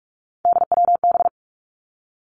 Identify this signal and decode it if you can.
Morse code, heard as a keyed tone, about 46 wpm, 700 Hz. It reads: DWB
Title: Three Challenging Letter-Number Combinations 45wpm